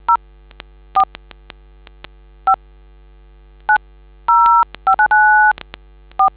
DTMF, heard as telephone touch tones, tones of about 100 ms. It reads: 04590594